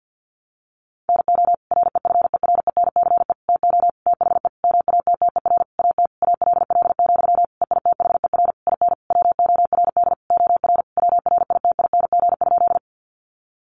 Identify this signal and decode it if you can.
Morse code, heard as a keyed tone, about 50 wpm, 700 Hz. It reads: NO REFERENCE TO THE MATTER AT ALL, EITHER IN WORD OR WRITING?
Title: Sentences from Top 1200 Words 50wpm